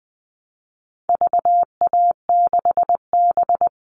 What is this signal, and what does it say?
Morse code, heard as a keyed tone, about 20 wpm, 700 Hz. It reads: VA6B
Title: All Cut Numbers - 4-Digit 20wpm